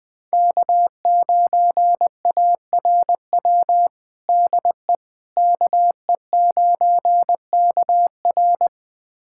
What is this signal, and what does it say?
Morse code, heard as a keyed tone, about 20 wpm, 700 Hz. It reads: K9ARW DE KE9KR
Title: USA Callsign DE Callsign - 5 Characters - 20wpm